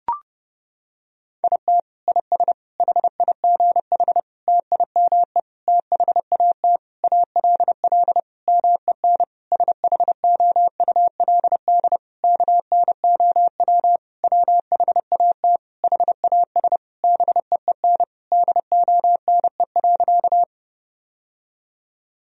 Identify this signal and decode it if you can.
Morse code, heard as a keyed tone, about 32 wpm, 700 Hz. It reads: IT IS HIGH TIME THAT ALL MEN SHOULD KNOW WHAT HAS BEEN DONE.
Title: Sentences from Top 1000 Words - Review Part 1 - 30wpm